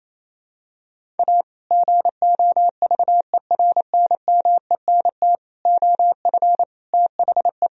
Morse code, heard as a keyed tone, about 27 wpm, 700 Hz. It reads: A GOVERNMENT OF THE
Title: Sentences from Top 1000 Words - Review Part 3 - 28wpm